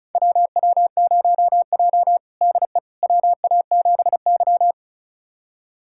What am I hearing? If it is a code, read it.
Morse code, heard as a keyed tone, about 36 wpm, 700 Hz. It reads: WW0J DE WA7Y
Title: USA Callsign DE Callsign - 4 Characters - 35wpm